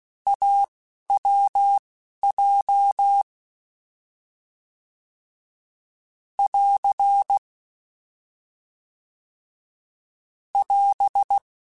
Morse code, heard as a keyed tone, about 16 wpm, spaced as for 11 wpm, 800 Hz. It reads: AWJ <AR> <AS>
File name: Short Runs / Short Runs-mp3s / AWJ @10